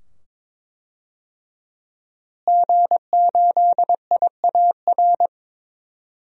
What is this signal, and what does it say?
Morse code, heard as a keyed tone, about 21 wpm, 700 Hz. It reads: G8IAR
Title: DXCC-100 Call Signs - 5 Characters - Repeated 2x - 22wpm